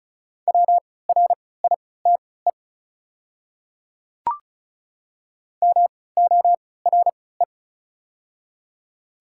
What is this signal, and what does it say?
Morse code, heard as a keyed tone, about 36 wpm, spaced as for 20 wpm, 700 Hz. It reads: WRITE MORE
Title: 3x Character Spacing - Top 100 Words - 35wpm